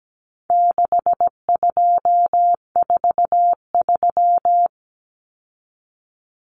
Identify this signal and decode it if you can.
Morse code, heard as a keyed tone, about 17 wpm, 700 Hz. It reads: 6243
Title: Numbers - 4-Digit 17wpm